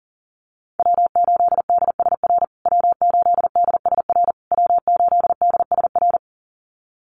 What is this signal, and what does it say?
Morse code, heard as a keyed tone, about 40 wpm, 700 Hz. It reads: W8DSR W8DSR W8DSR
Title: USA Call Signs Repeated 3x 40wpm